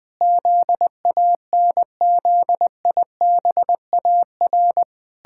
Morse code, heard as a keyed tone, about 20 wpm, 700 Hz. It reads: ZANZIBAR
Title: Pangram Challenge v2 - 20wpm